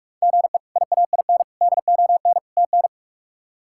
Morse code, heard as a keyed tone, about 46 wpm, 700 Hz. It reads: GE IAIN DON TN
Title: K1USN SST QSO Practice - 45wpm